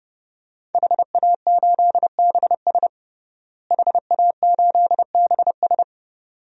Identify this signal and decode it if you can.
Morse code, heard as a keyed tone, about 30 wpm, 700 Hz. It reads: HA8BS HA8BS
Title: DXCC-100 Call Signs - 5 Characters - Repeated 2x - 30wpm